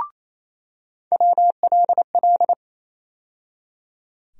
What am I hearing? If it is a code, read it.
Morse code, heard as a keyed tone, about 27 wpm, 700 Hz. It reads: WLL
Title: USA Call Sign Postfixes - 3 Letters 28wpm